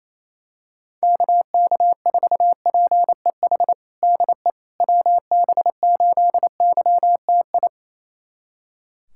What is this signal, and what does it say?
Morse code, heard as a keyed tone, about 27 wpm, 700 Hz. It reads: KK4PEH DE WB8YTI
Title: USA Callsign DE Callsign - 6 Characters - 28wpm